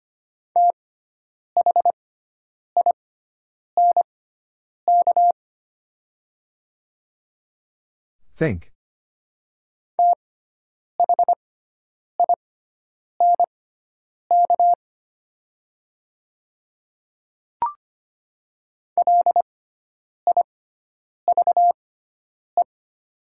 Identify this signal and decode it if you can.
Morse code, heard as a keyed tone, about 25 wpm, spaced as for 9 wpm, 700 Hz. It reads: THINK THINK LIVE
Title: 6x Character Spacing - Top 300 Words - 25wpm